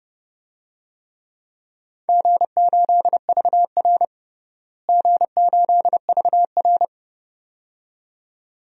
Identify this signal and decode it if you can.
Morse code, heard as a keyed tone, about 30 wpm, 700 Hz. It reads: G8VR G8VR
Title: DXCC-100 Call Signs - 4 Characters - Repeated 2x - 30wpm